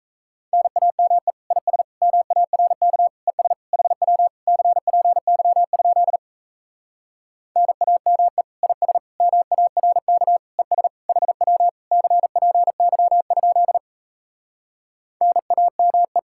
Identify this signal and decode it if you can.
Morse code, heard as a keyed tone, about 38 wpm, 700 Hz. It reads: NAME IS MARK ES HW CPY? NAME IS MARK ES HW CPY? NAME